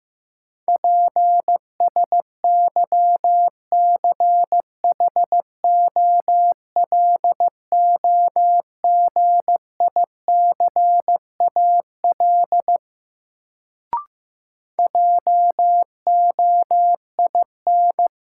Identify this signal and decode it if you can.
Morse code, heard as a keyed tone, about 15 wpm, 700 Hz. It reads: PSYCHOLOGICAL JOIN
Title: Sentences from Top 1400 Words - New Word Drill Alt - 15wpm